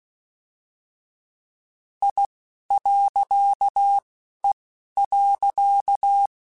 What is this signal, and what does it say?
Morse code, heard as a keyed tone, about 16 wpm, spaced as for 11 wpm, 800 Hz. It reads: I.E.
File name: Technical Subjects / Technical Subjects-mp3s / Reactance @10